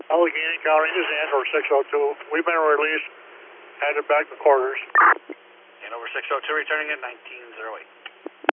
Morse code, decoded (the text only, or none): none